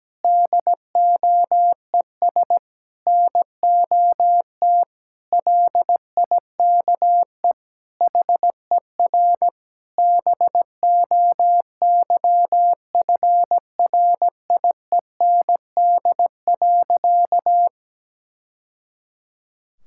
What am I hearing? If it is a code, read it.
Morse code, heard as a keyed tone, about 17 wpm, 700 Hz. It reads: DOES NOT LIKE HER BOYFRIEND.
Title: Sentences from Top 1000 Words - Review Part 5 - 17wpm